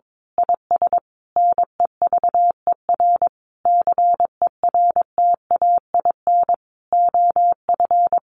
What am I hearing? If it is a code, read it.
Morse code, heard as a keyed tone, about 22 wpm, 700 Hz. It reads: IS NEVER CERTAIN OF
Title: Sentences from Top 1300 Words 22wpm